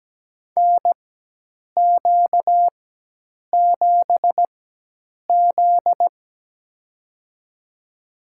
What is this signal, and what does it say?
Morse code, heard as a keyed tone, about 17 wpm, spaced as for 8 wpm, 700 Hz. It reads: NQ7Z
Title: USA Call Signs - 4 Characters - 4x Character Spacing - 17wpm